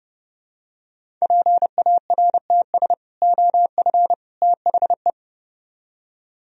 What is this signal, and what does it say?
Morse code, heard as a keyed tone, about 30 wpm, 700 Hz. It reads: PARTS OF THE